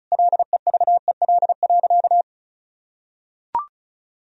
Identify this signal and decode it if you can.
Morse code, heard as a keyed tone, about 36 wpm, 700 Hz. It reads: LEVEL.
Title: Sentences from Top 1000 Words - Review Part 4 - 35wpm